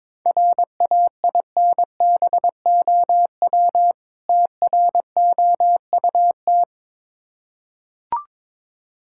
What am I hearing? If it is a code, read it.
Morse code, heard as a keyed tone, about 23 wpm, 690 Hz. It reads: RAINBOW TROUT
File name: Difficult Words with Hints - 22wpm